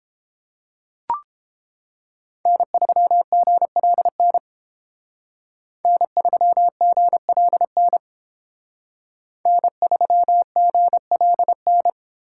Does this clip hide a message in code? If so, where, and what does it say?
Yes: Morse code from 2.5 to 11.9 s, heard as a keyed tone, about 31 wpm, 700 Hz. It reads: N3GLN N3GLN N3GLN